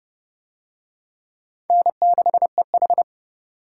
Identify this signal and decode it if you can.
Morse code, heard as a keyed tone, about 29 wpm, 700 Hz. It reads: N6EH